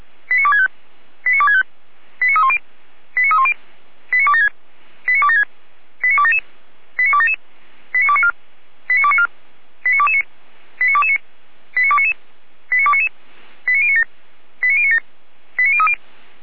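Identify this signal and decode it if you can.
Five-tone selective call (ZVEI), heard as quick runs of five tones, tones of about 70 ms. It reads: 89265 89265 89310 89310 89276 89276 89277 89277 89294 89294 89229 89229 89299 89299 89097 89097 89030
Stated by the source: ZVEI five-tone selective call received on an SDR.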